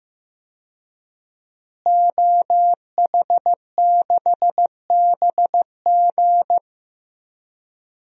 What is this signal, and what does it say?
Morse code, heard as a keyed tone, about 15 wpm, 700 Hz. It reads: OH6BG
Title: DXCC-100 Call Signs - 5 Characters - Repeated 2x - 15wpm